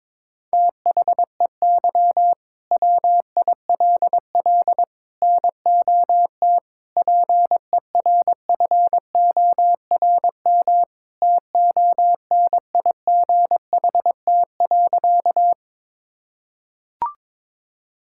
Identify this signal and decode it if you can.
Morse code, heard as a keyed tone, about 22 wpm, 700 Hz. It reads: THEY WILL NOT PERFORM TONIGHT.
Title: Sentences from Top 1000 Words - Review Part 5 - 22wpm